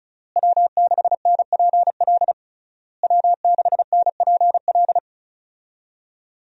Morse code, heard as a keyed tone, about 34 wpm, 700 Hz. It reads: W6NPL W6NPL
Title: USA Call Signs - 5 Characters - Repeated 3x - 35wpm